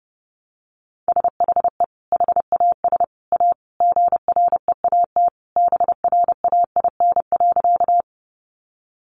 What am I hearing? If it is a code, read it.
Morse code, heard as a keyed tone, about 30 wpm, 700 Hz. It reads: SHE HAS A GREAT BRAIN.
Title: Sentences from Top 1100 Words 30wpm